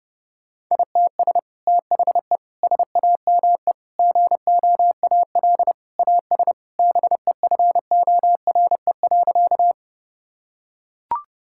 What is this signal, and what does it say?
Morse code, heard as a keyed tone, about 30 wpm, 700 Hz. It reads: ITS THE SAME GOAL AS BEFORE.
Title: Sentences from Top 1000 Words - Review Part 5 - 30wpm